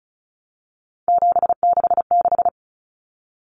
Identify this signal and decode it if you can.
Morse code, heard as a keyed tone, about 34 wpm, 700 Hz. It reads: 766
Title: Numbers - 3-Digit 35wpm